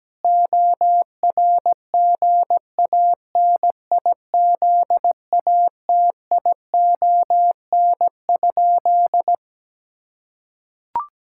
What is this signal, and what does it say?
Morse code, heard as a keyed tone, about 17 wpm, 700 Hz. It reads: ORGANIZATION?
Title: Sentences from Top 1000 Words - Review Part 5 - 17wpm